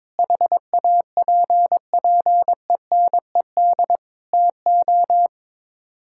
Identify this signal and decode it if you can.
Morse code, heard as a keyed tone, about 22 wpm, 700 Hz. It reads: HAPPENED TO
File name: practice / 22wpm / Sets of 7 Words - 22wpm